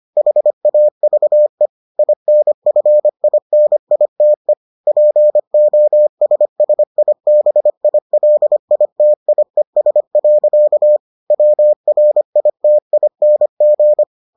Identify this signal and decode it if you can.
Morse code, heard as a keyed tone, about 25 wpm, 600 Hz. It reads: HAVE INFINITE POSSIBILITIES. WRITING